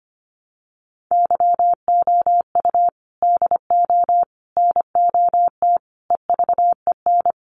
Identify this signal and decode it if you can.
Morse code, heard as a keyed tone, about 25 wpm, 700 Hz. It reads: YOU DO NOT EVEN
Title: Sets of 5 Words Encore2 25wpm